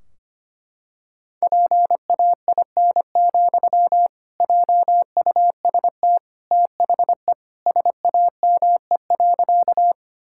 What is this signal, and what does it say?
Morse code, heard as a keyed tone, about 25 wpm, 700 Hz. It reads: PAIN, JUST THE SAME.
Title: Sentences from Top 1600 Words 25wpm